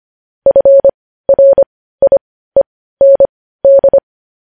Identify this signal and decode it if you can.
Morse code, heard as a keyed tone, about 25 wpm, spaced as for 15 wpm, 550 Hz. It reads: FRIEND